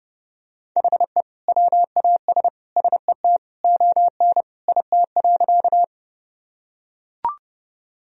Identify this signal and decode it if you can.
Morse code, heard as a keyed tone, about 30 wpm, 700 Hz. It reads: HE WAS SET ON IT.
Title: Sentences from Top 1000 Words - Review Part 1 - 30wpm